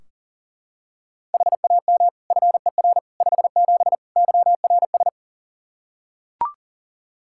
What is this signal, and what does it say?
Morse code, heard as a keyed tone, about 40 wpm, 700 Hz. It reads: HAM FER 57 YRS
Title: K1USN SST QSO Practice 40wpm